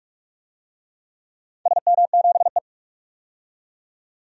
Morse code, heard as a keyed tone, about 44 wpm, 700 Hz. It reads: SM7E
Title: DXCC-100 Call Signs - 4 Characters - 45wpm